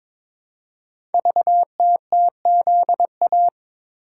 Morse code, heard as a keyed tone, about 23 wpm, 700 Hz. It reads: VTTZA
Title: Code Groups - 5 Letters - 22wpm